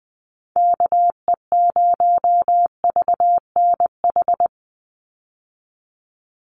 Morse code, heard as a keyed tone, about 20 wpm, 700 Hz. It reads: KE0VNH
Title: USA Call Signs 20wpm